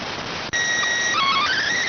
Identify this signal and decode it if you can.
SSTV (slow-scan television), PD 290.